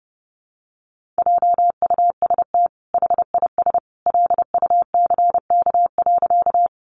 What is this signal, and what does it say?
Morse code, heard as a keyed tone, about 30 wpm, 700 Hz. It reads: JUST HIS LUCK.
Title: Sentences from Top 1500 Words 30wpm